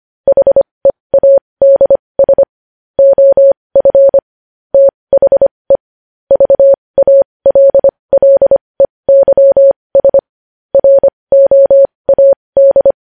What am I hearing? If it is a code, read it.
Morse code, heard as a keyed tone, about 25 wpm, spaced as for 20 wpm, 550 Hz. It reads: HEADS OF THE VALLEYS ROAD
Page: News headlines in Morse code at 20 wpm